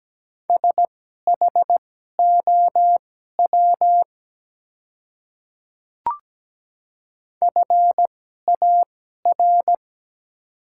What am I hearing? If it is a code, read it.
Morse code, heard as a keyed tone, about 17 wpm, spaced as for 12 wpm, 700 Hz. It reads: SHOW FAR
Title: 2x Character Spacing - Top 300 Words - 17wpm